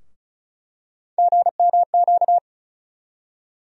Morse code, heard as a keyed tone, about 36 wpm, 700 Hz. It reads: GMQ
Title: USA Call Sign Postfixes - 3 Letters 35wpm